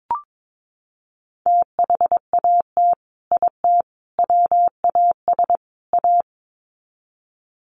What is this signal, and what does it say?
Morse code, heard as a keyed tone, about 22 wpm, 700 Hz. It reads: THAT IT WAS A
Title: Sets of 4 Words 22wpm